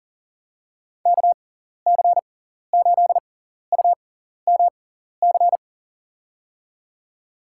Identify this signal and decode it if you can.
Morse code, heard as a keyed tone, about 40 wpm, spaced as for 14 wpm, 700 Hz. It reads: KC8UMC
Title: USA Call Signs - 6 Characters - 6x Character Spacing - 40wpm